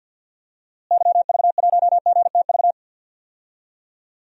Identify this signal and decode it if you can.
Morse code, heard as a keyed tone, about 53 wpm, 700 Hz. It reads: YV1GTV
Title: DXCC-100 Call Signs - 6 Characters - Repeated 2x - 50wpm